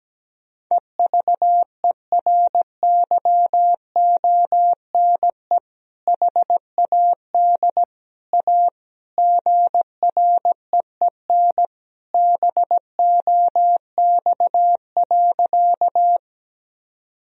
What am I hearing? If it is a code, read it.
Morse code, heard as a keyed tone, about 17 wpm, 700 Hz. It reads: EVERYONE HAD A GREEN BOX.
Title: Sentences from Top 1000 Words - Review Part 2 - 17wpm